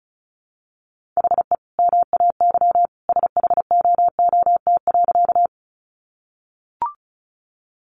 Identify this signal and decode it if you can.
Morse code, heard as a keyed tone, about 34 wpm, 700 Hz. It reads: HE MAY SHOOT.
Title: Sentences from Top 1200 Words 35wpm